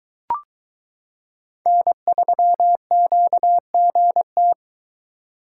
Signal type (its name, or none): Morse code